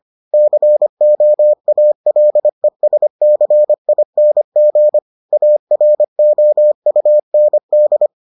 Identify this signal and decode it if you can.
Morse code, heard as a keyed tone, about 25 wpm, 600 Hz. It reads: COALESCING AROUND